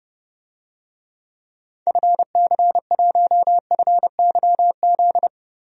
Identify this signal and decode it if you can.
Morse code, heard as a keyed tone, about 30 wpm, 700 Hz. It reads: FC1FYZ
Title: DXCC-100 Call Signs - 6 Characters - 30wpm